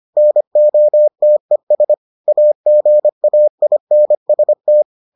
Morse code, heard as a keyed tone, about 25 wpm, 600 Hz. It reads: NOTES AGAINST